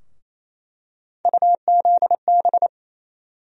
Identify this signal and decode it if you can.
Morse code, heard as a keyed tone, about 28 wpm, 700 Hz. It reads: UZB
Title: USA Call Sign Postfixes - 3 Letters 28wpm